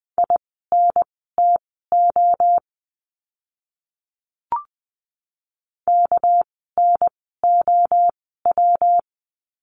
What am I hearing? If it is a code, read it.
Morse code, heard as a keyed tone, about 20 wpm, spaced as for 15 wpm, 700 Hz. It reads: INTO KNOW
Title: 244 Lesson 39f - Letter z - Words 20wpm